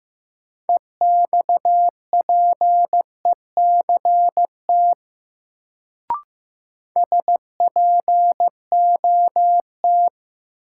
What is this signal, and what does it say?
Morse code, heard as a keyed tone, about 15 wpm, 700 Hz. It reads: EXPECT SPOT
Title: Sentences from Top 1100 Words - New Word Drill - 15wpm